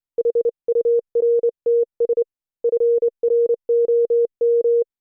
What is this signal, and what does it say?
Morse code, heard as a keyed tone, about 28 wpm, 470 Hz. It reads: HURTS FROM